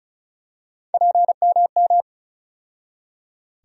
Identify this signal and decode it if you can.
Morse code, heard as a keyed tone, about 34 wpm, 700 Hz. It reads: PMM
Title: USA Call Sign Postfixes - 3 Letters 35wpm